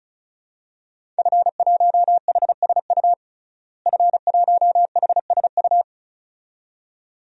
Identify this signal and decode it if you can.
Morse code, heard as a keyed tone, about 36 wpm, 700 Hz. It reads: F1HSU F1HSU
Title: DXCC-100 Call Signs - 5 Characters - Repeated 2x - 35wpm